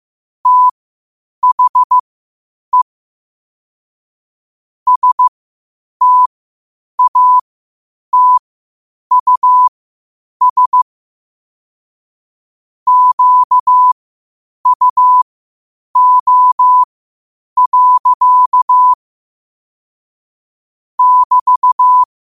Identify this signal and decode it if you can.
Morse code, heard as a keyed tone, about 15 wpm, spaced as for 8 wpm, 1000 Hz. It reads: THE STATUS QUO. <BT>
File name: qotdfiles / 8wpm